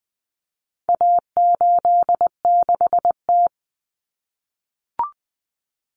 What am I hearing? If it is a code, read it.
Morse code, heard as a keyed tone, about 20 wpm, 700 Hz. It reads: A86T
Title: Common Cut Numbers - 4-Digit 20wpm